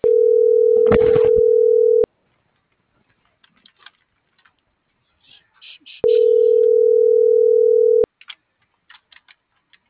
Ringback tone (North America), 440 + 480 Hz.